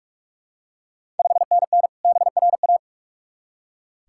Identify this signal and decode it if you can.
Morse code, heard as a keyed tone, about 46 wpm, 690 Hz. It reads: 5NN BRA
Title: Ontario QSO Party Exchange 45wpm